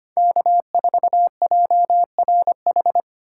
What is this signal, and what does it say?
Morse code, heard as a keyed tone, about 25 wpm, 700 Hz. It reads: K4JRH
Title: USA Call Signs - 5 Characters - 3x Character Spacing - 25wpm